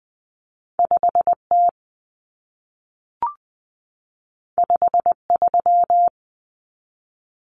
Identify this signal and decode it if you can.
Morse code, heard as a keyed tone, about 20 wpm, 700 Hz. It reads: 5T 53
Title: Common Cut Numbers - 2-Digit 20wpm